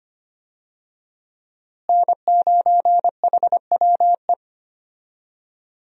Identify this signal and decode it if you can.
Morse code, heard as a keyed tone, about 25 wpm, 700 Hz. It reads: N9HWE
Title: USA Call Signs - 5 Characters - Repeated 2x - 25wpm